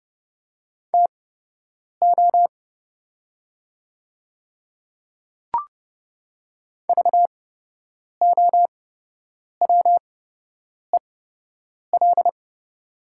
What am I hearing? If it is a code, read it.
Morse code, heard as a keyed tone, about 30 wpm, spaced as for 8 wpm, 700 Hz. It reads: TO VOWEL
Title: 8x Character Spacing - Top 500 Words - 30wpm